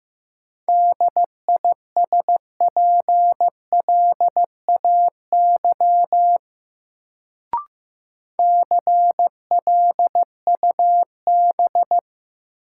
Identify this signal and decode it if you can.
Morse code, heard as a keyed tone, about 15 wpm, 700 Hz. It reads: DISPLAY CLUB
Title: Sentences from Top 1200 Words - New Word Drill - 15wpm